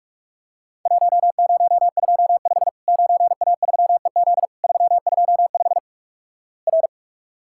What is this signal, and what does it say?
Morse code, heard as a keyed tone, about 46 wpm, 700 Hz. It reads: 1025 9A3EZ 325 R
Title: Worked All Europe DX Contest - QTC Exchange - 45wpm